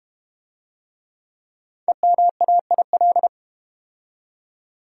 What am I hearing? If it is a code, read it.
Morse code, heard as a keyed tone, about 32 wpm, 700 Hz. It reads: EMAIL